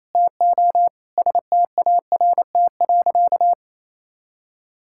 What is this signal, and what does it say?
Morse code, heard as a keyed tone, about 28 wpm, 700 Hz. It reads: TO START.
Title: Sentences from Top 1000 Words - Review Part 5 - 28wpm